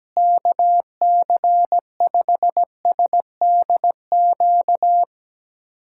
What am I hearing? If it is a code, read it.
Morse code, heard as a keyed tone, about 17 wpm, 700 Hz. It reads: KC5SDQ
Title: USA Callsign DE Callsign - 6 Characters - 17wpm